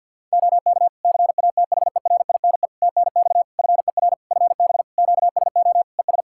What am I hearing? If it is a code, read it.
Morse code, heard as a keyed tone, about 51 wpm, 700 Hz. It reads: OK CATHERINE TNX FER FB QSO ES
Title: Practice QSO - Exchange 4 - Ending - 50wpm